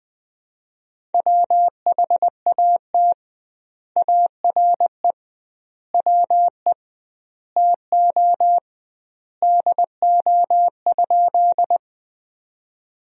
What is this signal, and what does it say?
Morse code, heard as a keyed tone, about 21 wpm, 700 Hz. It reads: WHAT ARE WE TO DO?
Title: 2x Word-Spacing - Sentences from Top 100 Words - 20wpm